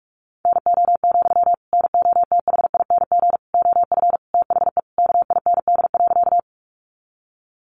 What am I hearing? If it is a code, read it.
Morse code, heard as a keyed tone, about 44 wpm, 700 Hz. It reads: NO, NOTHING OF THE KIND.
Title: Sentences from Top 500 Words 45wpm